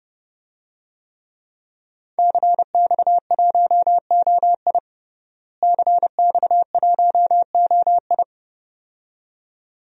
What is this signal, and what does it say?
Morse code, heard as a keyed tone, about 29 wpm, 700 Hz. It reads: CX1OI CX1OI
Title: DXCC-100 Call Signs - 5 Characters - Repeated 2x - 30wpm